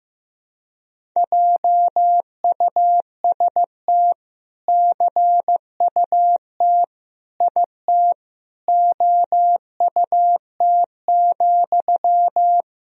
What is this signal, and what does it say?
Morse code, heard as a keyed tone, about 16 wpm, 700 Hz. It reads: JUST CUT IT OUT,
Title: Sentences from Top 300 Words and Abbreviations 15wpm